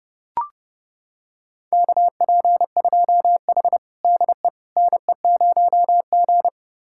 Morse code, heard as a keyed tone, about 30 wpm, 700 Hz. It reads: KP2H DE NE0G
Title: USA Callsign DE Callsign - 4 Characters - 30wpm